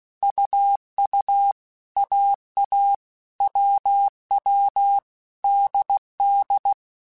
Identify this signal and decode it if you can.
Morse code, heard as a keyed tone, about 16 wpm, 800 Hz. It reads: UU AA WW DD
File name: Basic Letter Groups By7Groups / AWUNDG / AWUNDG@15